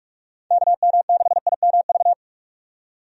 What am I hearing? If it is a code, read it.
Morse code, heard as a keyed tone, about 46 wpm, 700 Hz. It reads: KM6IMV
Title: USA Call Signs - 6 Characters - 45wpm